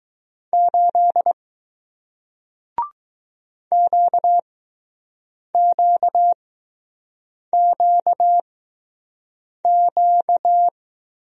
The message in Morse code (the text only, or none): 8 QQQQ